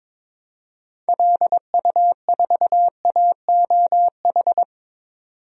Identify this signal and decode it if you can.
Morse code, heard as a keyed tone, about 22 wpm, 700 Hz. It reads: LU4AOH
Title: DXCC-100 Call Signs - 6 Characters - 22wpm